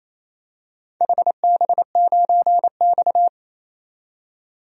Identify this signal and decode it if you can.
Morse code, heard as a keyed tone, about 28 wpm, 700 Hz. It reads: HB9X
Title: DXCC-100 Call Signs - 4 Characters - 28wpm